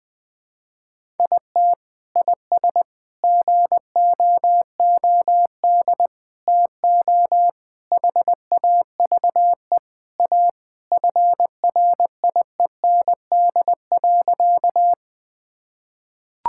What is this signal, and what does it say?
Morse code, heard as a keyed tone, about 20 wpm, 700 Hz. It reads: IT IS GOOD TO HAVE A FRIEND.
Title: Sentences with 3-Word Phrases - Spoken Beginning - 20wpm